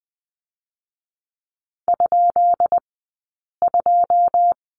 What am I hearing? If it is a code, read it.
Morse code, heard as a keyed tone, about 20 wpm, 700 Hz. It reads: ? 2
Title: 132 Lesson 22d - Question Mark - Two Characters 20wpm